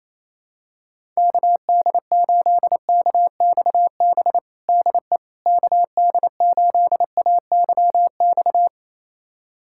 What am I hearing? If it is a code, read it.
Morse code, heard as a keyed tone, about 28 wpm, 700 Hz. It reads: KD8KXB DE KD8AYX
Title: USA Callsign DE Callsign 28wpm